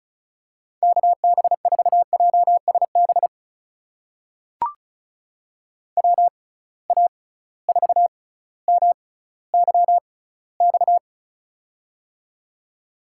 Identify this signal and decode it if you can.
Morse code, heard as a keyed tone, about 34 wpm, spaced as for 18 wpm, 700 Hz. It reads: KB4JSB WA4MYX
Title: USA Call Signs - 6 Characters - 6x Character Spacing - 35wpm